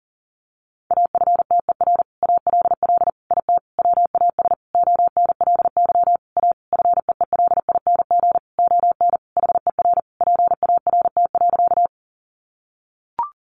Morse code, heard as a keyed tone, about 40 wpm, 700 Hz. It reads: AFTER ALL IT WAS ONLY A FEELING ON HER PART.
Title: Sentences from Top 1200 Words 40wpm